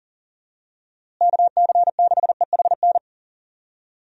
Morse code, heard as a keyed tone, about 39 wpm, 700 Hz. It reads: KC6EHN
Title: USA Call Signs - 6 Characters - Repeated 3x - 40wpm